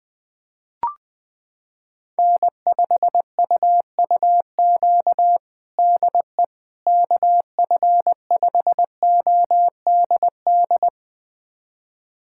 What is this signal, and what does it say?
Morse code, heard as a keyed tone, about 20 wpm, 700 Hz. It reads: N5UUQ DE KF5ODD
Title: USA Callsign DE Callsign 20wpm